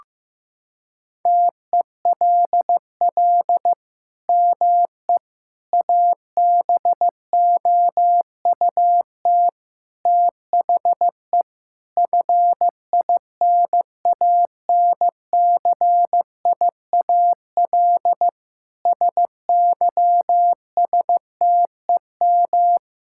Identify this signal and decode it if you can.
Morse code, heard as a keyed tone, about 15 wpm, 700 Hz. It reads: TELL ME ABOUT THE FINANCIAL SYSTEM
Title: Sentences from Top 1000 Words - Review Part 5 - 15wpm